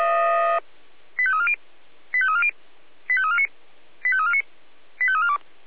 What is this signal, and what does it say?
Five-tone selective call (ZVEI), heard as quick runs of five tones, tones of about 70 ms. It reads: 85340 85340 85349 85349 85342
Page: ZVEI